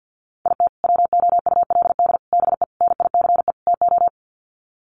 Morse code, heard as a keyed tone, about 50 wpm, 700 Hz. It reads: IT WOULD BE NICE TO
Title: Sets of 5 Words Encore2 50wpm